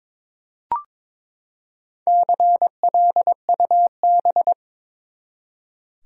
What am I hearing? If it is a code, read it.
Morse code, heard as a keyed tone, about 23 wpm, 700 Hz. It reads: CLUB
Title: Four Letter Words - 22wpm